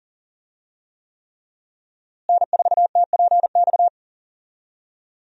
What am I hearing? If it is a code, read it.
Morse code, heard as a keyed tone, about 39 wpm, 700 Hz. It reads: N4TPX
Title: USA Call Signs - 5 Characters - Repeated 3x - 40wpm